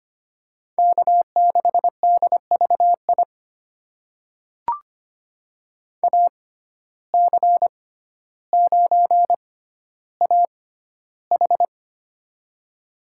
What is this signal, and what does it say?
Morse code, heard as a keyed tone, about 25 wpm, spaced as for 13 wpm, 700 Hz. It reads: K6DVI AC9AH
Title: USA Call Signs - 5 Characters - 6x Character Spacing - 25wpm